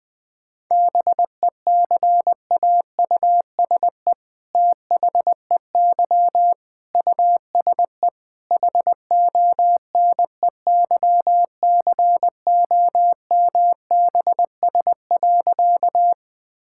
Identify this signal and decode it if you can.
Morse code, heard as a keyed tone, about 20 wpm, 700 Hz. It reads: BECAUSE THEY USE HONEYCOMBS.
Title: All-Age Appropriate Jokes - 20wpm